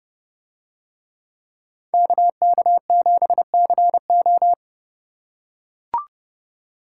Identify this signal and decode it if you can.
Morse code, heard as a keyed tone, about 31 wpm, 700 Hz. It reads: KK7CO